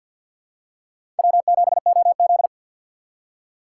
Morse code, heard as a keyed tone, about 46 wpm, 690 Hz. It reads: W7OZ
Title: USA Call Signs - 2x Character Spacing - 50wpm